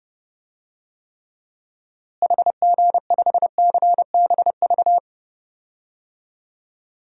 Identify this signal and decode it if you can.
Morse code, heard as a keyed tone, about 30 wpm, 700 Hz. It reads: HG5CBV
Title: DXCC-100 Call Signs - 6 Characters - Repeated 2x - 30wpm